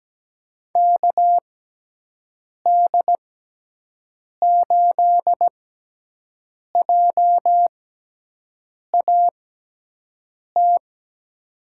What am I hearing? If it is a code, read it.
Morse code, heard as a keyed tone, about 17 wpm, spaced as for 6 wpm, 700 Hz. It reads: KD8JAT
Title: USA Call Signs - 6x Character Spacing - 17wpm